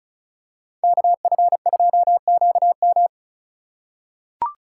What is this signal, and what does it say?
Morse code, heard as a keyed tone, about 36 wpm, 700 Hz. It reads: KF2QM
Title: USA Call Signs - 5 Characters - 4x Character Spacing - 35wpm